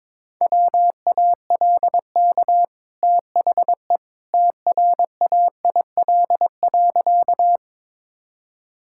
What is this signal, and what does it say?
Morse code, heard as a keyed tone, about 22 wpm, 700 Hz. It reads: WALK THE TRAIL.
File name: practice / 22wpm / Easy Sentences with Hints - 22wpm